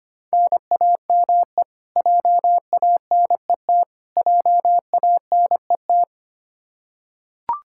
Morse code, heard as a keyed tone, about 25 wpm, 700 Hz. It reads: NAME JANET JANET
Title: Practice QSO - Exchange 1 - Beginning - 25wpm